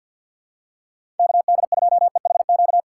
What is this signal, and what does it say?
Morse code, heard as a keyed tone, about 48 wpm, 700 Hz. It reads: KD2EHQ
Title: USA Call Signs - 6 Characters - 3x Character Spacing - 50wpm